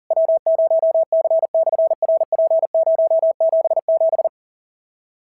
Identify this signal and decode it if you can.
Morse code, heard as a keyed tone, about 43 wpm, 650 Hz. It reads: W0C/RP077
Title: SOTA QSO Practice 40wpm